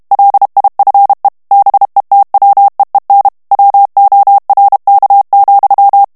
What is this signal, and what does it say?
Morse code, heard as a keyed tone, about 32 wpm, 800 Hz. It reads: LIFE BETWEEN WORK,
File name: morse mp3 / ANGLAIS texte clair 1 vitesse 30 mots minute